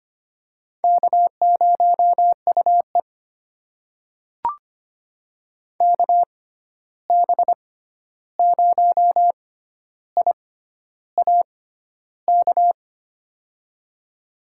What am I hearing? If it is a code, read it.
Morse code, heard as a keyed tone, about 25 wpm, spaced as for 11 wpm, 700 Hz. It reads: K0UE KB0IAK
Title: USA Call Signs - 6x Character Spacing - 25wpm